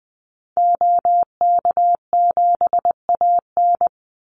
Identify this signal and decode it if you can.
Morse code, heard as a keyed tone, about 20 wpm, 700 Hz. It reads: OK7AN
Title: Call Signs Repeated 2x - 20wpm